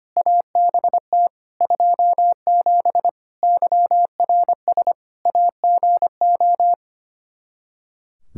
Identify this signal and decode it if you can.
Morse code, heard as a keyed tone, about 25 wpm, 700 Hz. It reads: ABT 27 YRS AGO
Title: Practice QSO - Exchange 3 - More Info - 25wpm